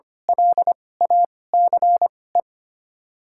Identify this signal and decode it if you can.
Morse code, heard as a keyed tone, about 25 wpm, spaced as for 18 wpm, 700 Hz. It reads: LACE
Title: 2x Character Spacing - Top 200 Words - 25wpm